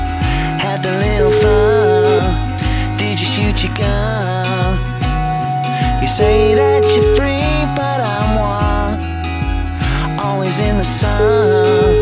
Ringback tone (Europe, 425 Hz), with noise.